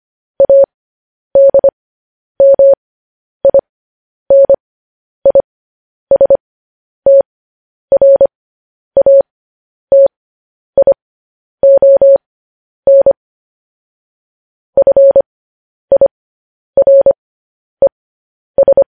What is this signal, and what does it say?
Morse code, heard as a keyed tone, about 25 wpm, spaced as for 10 wpm, 550 Hz. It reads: ADMINISTRATION FIRES
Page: news headlines in morse code at 10 wpm